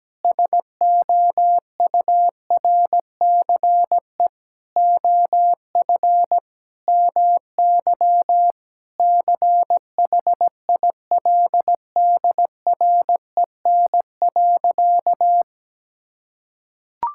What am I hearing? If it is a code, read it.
Morse code, heard as a keyed tone, about 17 wpm, 700 Hz. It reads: SOURCE OF MY CHILDREN.
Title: Sentences from Top 1000 Words - Review Part 5 - 17wpm